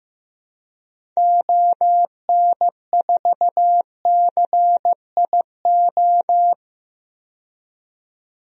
Morse code, heard as a keyed tone, about 15 wpm, 700 Hz. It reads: ON4CIO
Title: DXCC-100 Call Signs - 6 Characters - 15wpm